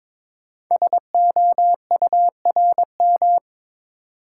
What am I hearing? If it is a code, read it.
Morse code, heard as a keyed tone, about 22 wpm, 700 Hz. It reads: SOURM
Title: Code Groups - 5 Letters - 22wpm